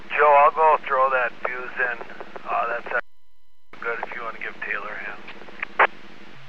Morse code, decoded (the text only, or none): none